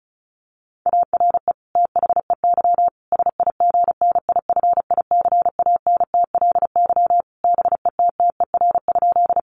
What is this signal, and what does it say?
Morse code, heard as a keyed tone, about 34 wpm, 700 Hz. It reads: ARE THEY SIGNIFICANTLY BETTER?
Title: Sentences from Top 1500 Words 35wpm